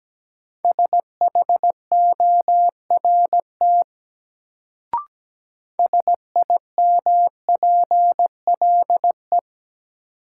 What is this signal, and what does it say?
Morse code, heard as a keyed tone, about 17 wpm, 700 Hz. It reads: SHORT SIMPLE
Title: Sentences from Top 400 Words - New Word Drill - 17wpm